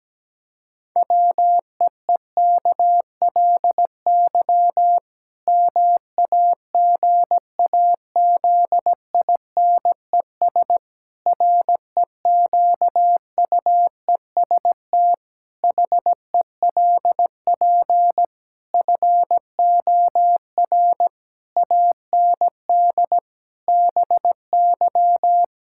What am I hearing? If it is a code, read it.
Morse code, heard as a keyed tone, about 17 wpm, 700 Hz. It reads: WEEKLY MAGAZINES REQUEST HELP FOR AND BY
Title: Pangram Challenge v2 - 17wpm